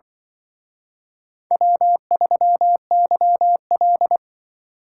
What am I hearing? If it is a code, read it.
Morse code, heard as a keyed tone, about 24 wpm, 700 Hz. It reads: W3YL
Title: Speed Racing - USA 1x2 Call Signs 20wpm